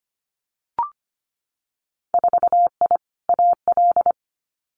Morse code, heard as a keyed tone, about 25 wpm, 700 Hz. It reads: 4I AL
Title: Winter Field Day Exchanges Encore 25wpm